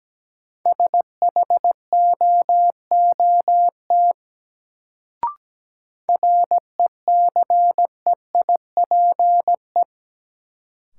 Morse code, heard as a keyed tone, about 17 wpm, 700 Hz. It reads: SHOOT RECEIPE
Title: Sentences from Top 1200 Words - New Word Drill Alt - 17wpm